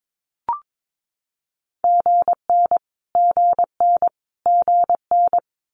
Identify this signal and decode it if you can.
Morse code, heard as a keyed tone, about 22 wpm, 700 Hz. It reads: GN GN GN
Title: Multi-letter Phonemes Repeated 3x 22wpm